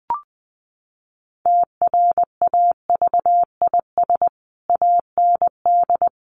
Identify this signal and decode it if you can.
Morse code, heard as a keyed tone, about 20 wpm, 700 Hz. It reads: TRAVIS AND
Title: A Sound Of Thunder 20wpm